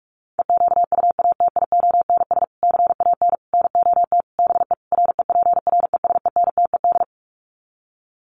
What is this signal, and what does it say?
Morse code, heard as a keyed tone, about 44 wpm, 700 Hz. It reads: EQUATIONS CAN NOT BE REPRESENTED
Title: Sets of 5 Words 45wpm